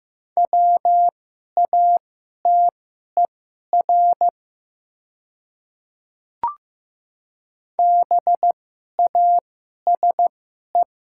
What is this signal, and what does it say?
Morse code, heard as a keyed tone, about 15 wpm, spaced as for 11 wpm, 700 Hz. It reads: WATER BASE
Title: 2x Character Spacing - Top 300 Words - 15wpm